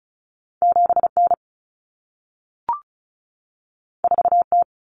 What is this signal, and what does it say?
Morse code, heard as a keyed tone, about 36 wpm, 700 Hz. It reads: 7N 4T
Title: Common Cut Numbers - 2-Digit 35wpm